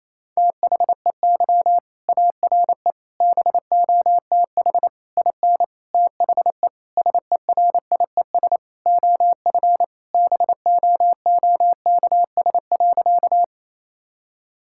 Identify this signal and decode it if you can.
Morse code, heard as a keyed tone, about 28 wpm, 700 Hz. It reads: THEY ARE BOTH IN THE SERIES OF BOOKS.
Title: Sentences from Top 1000 Words - Review Part 5 - 28wpm